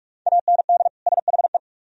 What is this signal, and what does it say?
Morse code, heard as a keyed tone, about 44 wpm, 700 Hz. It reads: AND SHE